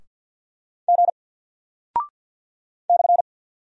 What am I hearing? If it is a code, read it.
Morse code, heard as a keyed tone, about 48 wpm, 700 Hz. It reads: G /
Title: Single Character (Letter:Number:Punctuation:Prosign) 50wpm